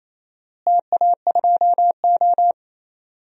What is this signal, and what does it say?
Morse code, heard as a keyed tone, about 27 wpm, 700 Hz. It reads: TA2O
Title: DXCC-100 Call Signs - 4 Characters - Repeated 2x - 28wpm